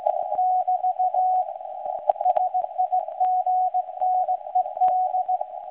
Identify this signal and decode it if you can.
Morse code, heard as a keyed tone, about 17 wpm, 740 Hz. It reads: <BT> SIGNED
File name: WebSDR records / SAQ recording start 17.2kHz